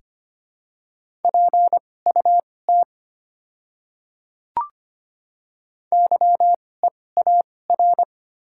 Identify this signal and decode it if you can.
Morse code, heard as a keyed tone, about 25 wpm, spaced as for 18 wpm, 700 Hz. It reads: PUT YEAR
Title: 2x Character Spacing - Top 200 Words - 25wpm